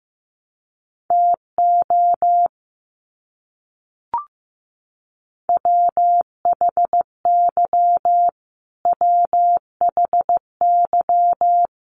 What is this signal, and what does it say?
Morse code, heard as a keyed tone, about 15 wpm, 700 Hz. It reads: TO WHY WHY
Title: Top 500 Words Repeated 15wpm